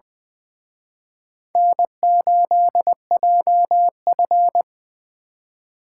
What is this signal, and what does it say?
Morse code, heard as a keyed tone, about 23 wpm, 700 Hz. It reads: N8JF
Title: Speed Racing - USA Call Signs 20wpm